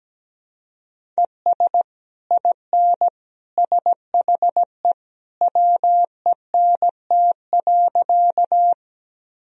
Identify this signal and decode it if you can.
Morse code, heard as a keyed tone, about 17 wpm, 700 Hz. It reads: ES IN SHE WENT.
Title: Sentences from Top 200 Words and Abbreviations 17wpm